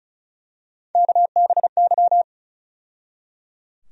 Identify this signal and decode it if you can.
Morse code, heard as a keyed tone, about 34 wpm, 700 Hz. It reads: KBY
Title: USA Call Sign Postfixes - 3 Letters 35wpm